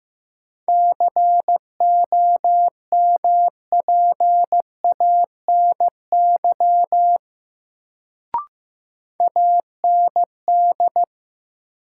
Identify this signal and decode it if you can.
Morse code, heard as a keyed tone, about 15 wpm, 700 Hz. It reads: COMPANY AND
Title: Sets of 2 Words - New Word Drill - 15wpm